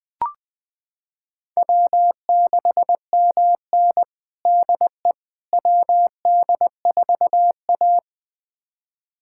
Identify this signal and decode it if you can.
Morse code, heard as a keyed tone, about 20 wpm, 700 Hz. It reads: W6MN DE WD4A
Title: USA Callsign DE Callsign - 4 Characters - 20wpm